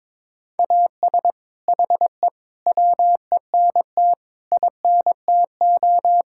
Morse code, heard as a keyed tone, about 22 wpm, 700 Hz. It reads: AS HE WENT INTO
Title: Sentences from Top 1600 Words 22wpm